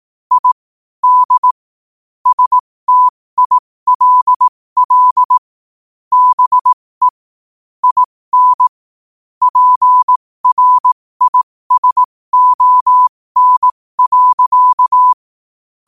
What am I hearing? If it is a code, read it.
Morse code, heard as a keyed tone, about 18 wpm, spaced as for 15 wpm, 1000 Hz. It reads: I D STILL BE IN PRISON.
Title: 15 WPM morse code quotes for Thu, 29 Feb 2024